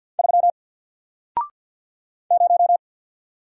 Morse code, heard as a keyed tone, about 50 wpm, 700 Hz. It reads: 3 0
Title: Single Letter-Number Alt 50wpm